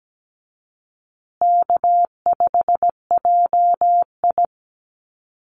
Morse code, heard as a keyed tone, about 17 wpm, 700 Hz. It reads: K5JI
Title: USA Call Signs Repeated 3x 17wpm